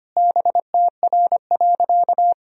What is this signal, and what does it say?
Morse code, heard as a keyed tone, about 25 wpm, 700 Hz. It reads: BTR.
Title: Sentences from Top 400 Words and Abbreviations 25wpm